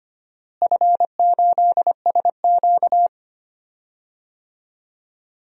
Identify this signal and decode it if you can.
Morse code, heard as a keyed tone, about 25 wpm, 700 Hz. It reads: F8SQ
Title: DXCC-100 Call Signs - 4 Characters - Repeated 2x - 25wpm